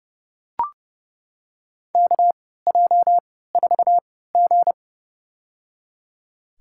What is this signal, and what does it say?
Morse code, heard as a keyed tone, about 30 wpm, spaced as for 17 wpm, 700 Hz. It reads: KJ4G
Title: USA Call Signs - 4 Characters - 3x Character Spacing - 30wpm